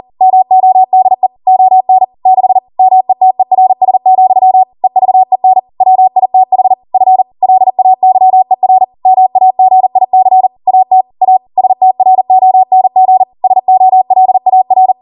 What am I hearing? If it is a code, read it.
Morse code, heard as a keyed tone, about 40 wpm, 750 Hz. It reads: MODE ON 6 METERS, EVEN WITH F LAYER MAGIC AT A STRONG SOLAR